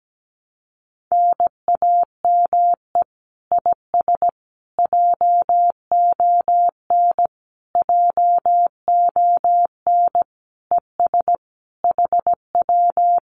Morse code, heard as a keyed tone, about 17 wpm, 700 Hz. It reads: NAME IS JON JON ES HW
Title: Common QSO Phrases Encore 17wpm